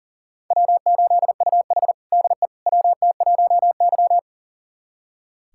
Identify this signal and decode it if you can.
Morse code, heard as a keyed tone, about 40 wpm, 700 Hz. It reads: W8UH DE WT1Y
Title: USA Callsign DE Callsign - 4 Characters - 40wpm